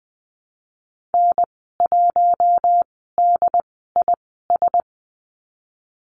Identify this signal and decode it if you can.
Morse code, heard as a keyed tone, about 20 wpm, spaced as for 15 wpm, 700 Hz. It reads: N1DIS
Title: 140 Lesson 23g - Letter f - Call Signs 20wpm